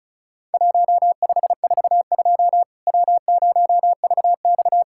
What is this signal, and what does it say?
Morse code, heard as a keyed tone, about 36 wpm, 700 Hz. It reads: 1542 W0VX
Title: Worked All Europe DX Contest - QTC Exchange - 35wpm